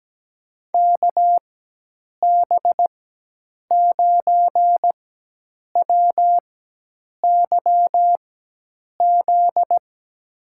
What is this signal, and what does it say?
Morse code, heard as a keyed tone, about 17 wpm, spaced as for 8 wpm, 700 Hz. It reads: KB9WYZ
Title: USA Call Signs - 4x Character Spacing - 17wpm